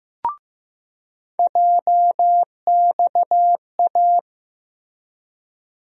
Morse code, heard as a keyed tone, about 15 wpm, 700 Hz. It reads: JXA
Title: USA Call Sign Postfixes - 3 Letters 15wpm